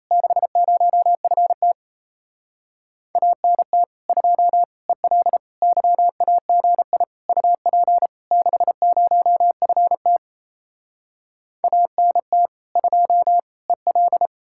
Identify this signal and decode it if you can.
Morse code, heard as a keyed tone, about 38 wpm, 700 Hz. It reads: 60FT ANT 2 EL YAGI UP 60FT ANT 2 EL